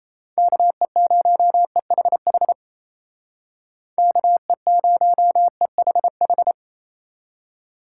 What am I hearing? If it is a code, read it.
Morse code, heard as a keyed tone, about 36 wpm, 700 Hz. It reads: KE0EHH KE0EHH